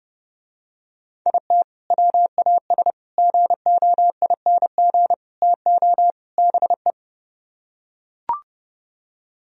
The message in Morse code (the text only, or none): IT WAS GOING TO BE